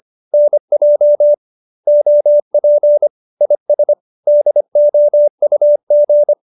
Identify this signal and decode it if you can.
Morse code, heard as a keyed tone, about 25 wpm, 590 Hz. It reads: NJ OP IS DOUG
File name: QSO220 25